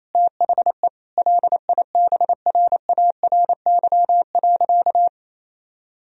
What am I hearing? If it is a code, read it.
Morse code, heard as a keyed tone, about 28 wpm, 700 Hz. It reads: THE LIBRARY.
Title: Sentences from Top 1000 Words - Review Part 3 - 28wpm